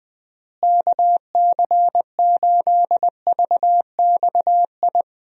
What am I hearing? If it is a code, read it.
Morse code, heard as a keyed tone, about 20 wpm, 700 Hz. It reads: KC8VXI
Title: USA Call Signs - 6 Characters - Repeated 3x - 20wpm